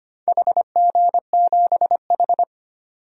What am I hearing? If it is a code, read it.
Morse code, heard as a keyed tone, about 25 wpm, 700 Hz. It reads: HG7H
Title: DXCC-100 Call Signs - 4 Characters - 25wpm